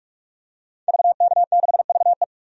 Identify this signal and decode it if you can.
Morse code, heard as a keyed tone, about 46 wpm, 700 Hz. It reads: VK6VE
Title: DXCC-100 Call Signs - 5 Characters - 45wpm